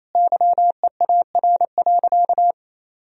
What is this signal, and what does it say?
Morse code, heard as a keyed tone, about 28 wpm, 700 Hz. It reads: YEAR.
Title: Sentences from Top 1000 Words - Review Part 1 - 28wpm